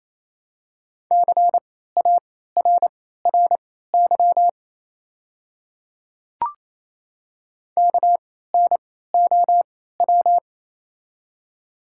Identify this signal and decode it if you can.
Morse code, heard as a keyed tone, about 28 wpm, spaced as for 16 wpm, 700 Hz. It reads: CARRY KNOW
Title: 3x Character Spacing - Top 500 Words - 28wpm